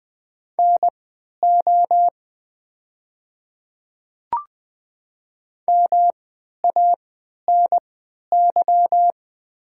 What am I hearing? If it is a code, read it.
Morse code, heard as a keyed tone, about 20 wpm, spaced as for 11 wpm, 700 Hz. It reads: NO MANY
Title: 3x Character Spacing - Top 100 Words - 20wpm